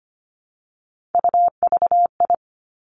Morse code, heard as a keyed tone, about 25 wpm, 700 Hz. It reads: UVI
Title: Three Challenging Letter-Number Combinations 25wpm